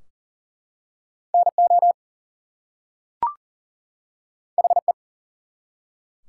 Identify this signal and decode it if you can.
Morse code, heard as a keyed tone, about 40 wpm, 700 Hz. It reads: NO HE
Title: Two Letter Words - 40wpm